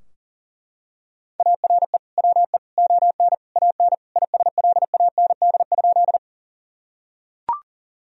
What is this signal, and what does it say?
Morse code, heard as a keyed tone, about 40 wpm, 700 Hz. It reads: ARE WE ON AN ISLAND?
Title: Sentences from Top 1000 Words - Review Part 3 - 40wpm